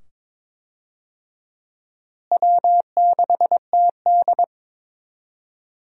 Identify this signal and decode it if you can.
Morse code, heard as a keyed tone, about 22 wpm, 700 Hz. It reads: W6TD
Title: USA Call Signs - 4 Characters - Repeated 2x - 22wpm